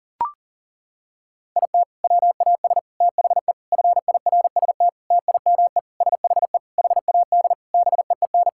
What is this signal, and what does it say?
Morse code, heard as a keyed tone, about 40 wpm, 700 Hz. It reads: IT WAS THE FIRST TIME SHE HAD BEEN
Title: Sentences from Top 1000 Words - Review Part 6 - 40wpm